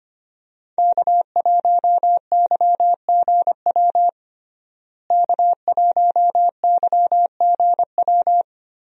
Morse code, heard as a keyed tone, about 25 wpm, 700 Hz. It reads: K1YGW K1YGW
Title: USA Call Signs - 5 Characters - Repeated 3x - 25wpm